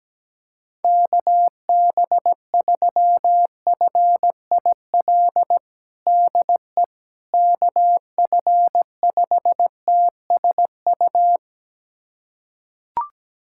Morse code, heard as a keyed tone, about 17 wpm, 700 Hz. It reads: KB3FIL DE KF5TSU
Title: USA Callsign DE Callsign - 6 Characters - 17wpm